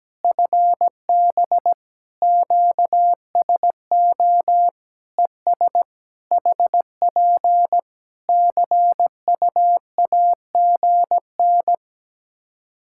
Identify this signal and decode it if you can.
Morse code, heard as a keyed tone, about 17 wpm, 700 Hz. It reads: FB QSO ES HP CUAGN
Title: Practice QSO - Exchange 4 - Ending - 17wpm